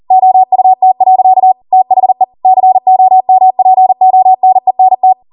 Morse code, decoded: OUT. THE COMPONENT